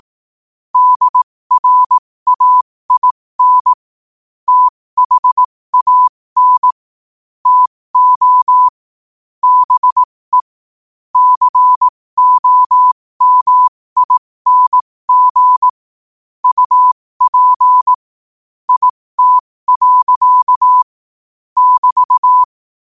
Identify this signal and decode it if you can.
Morse code, heard as a keyed tone, about 18 wpm, spaced as for 15 wpm, 1000 Hz. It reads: DRAIN THAN TO BE COMING UP IT. <BT>
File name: qotdfiles / 15wpm